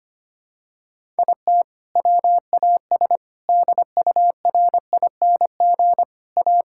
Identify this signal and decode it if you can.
Morse code, heard as a keyed tone, about 25 wpm, 700 Hz. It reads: IT WAS DURING A
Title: Sentences from Top 1000 Words - Review Part 5 - 25wpm